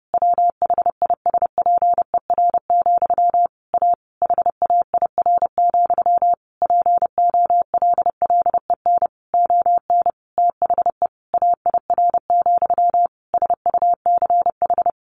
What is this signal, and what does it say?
Morse code, heard as a keyed tone, about 30 wpm, 700 Hz. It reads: WHISPER, A HAIR, POLLEN ON THE AIR, SUCH
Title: A Sound Of Thunder 30wpm